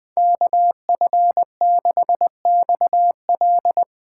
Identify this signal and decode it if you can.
Morse code, heard as a keyed tone, about 19 wpm, 700 Hz. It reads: KF6XL
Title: Speed Racing - USA 2x2 Call Signs 17wpm